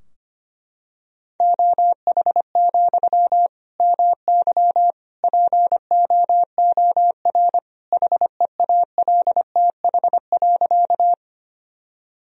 Morse code, heard as a keyed tone, about 25 wpm, 700 Hz. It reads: OH, MY POOR HEALTH.
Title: Sentences from Top 1000 Words - Review Part 3 - 25wpm